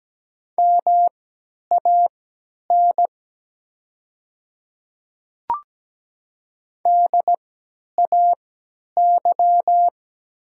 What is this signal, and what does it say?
Morse code, heard as a keyed tone, about 17 wpm, spaced as for 10 wpm, 700 Hz. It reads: MAN DAY
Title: 3x Character Spacing - Top 200 Words - 17wpm